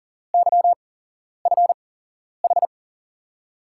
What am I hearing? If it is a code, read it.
Morse code, heard as a keyed tone, about 40 wpm, spaced as for 11 wpm, 700 Hz. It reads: YFH
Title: USA Call Signs - 8x Character Spacing - 40wpm